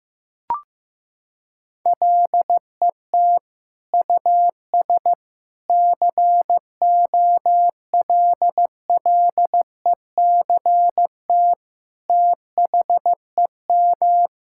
Morse code, heard as a keyed tone, about 15 wpm, 700 Hz. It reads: LET US COLLECT THEM
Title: Sentences from Top 1600 Words 15wpm